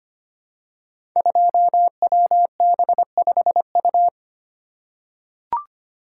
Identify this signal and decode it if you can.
Morse code, heard as a keyed tone, about 25 wpm, 700 Hz. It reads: 2WB5U
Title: Code Groups - 5 Letter:Numbers - 25wpm